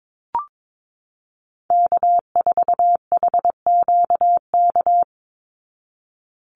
Morse code, heard as a keyed tone, about 22 wpm, 690 Hz. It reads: K4HQK
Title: Call Signs Repeated 2x - 22wpm